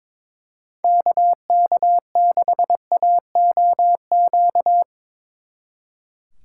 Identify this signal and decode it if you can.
Morse code, heard as a keyed tone, about 25 wpm, 700 Hz. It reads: KK6AOQ
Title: Speed Racing - USA 2x3 Call Signs 22wpm